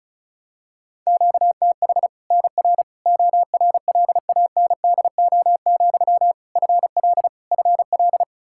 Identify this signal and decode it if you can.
Morse code, heard as a keyed tone, about 36 wpm, 700 Hz. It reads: QTH NR ORLANDO, FL FL
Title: Practice QSO - Exchange 1 - Beginning - 35wpm